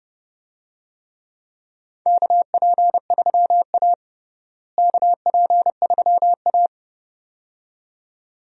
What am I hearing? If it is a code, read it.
Morse code, heard as a keyed tone, about 30 wpm, 700 Hz. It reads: KP3A KP3A
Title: USA Call Signs - 4 Characters - Repeated 2x - 30wpm